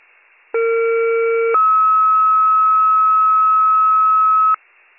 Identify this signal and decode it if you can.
Two-tone page, heard as a one-second tone, then a three-second tone, about 460 then 1290 Hz. It is heard from 0.5 to 4.5 s.